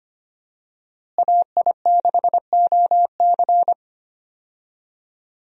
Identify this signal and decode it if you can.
Morse code, heard as a keyed tone, about 25 wpm, 700 Hz. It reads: AI6OC
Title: USA Call Signs - 5 Characters - Repeated 2x - 25wpm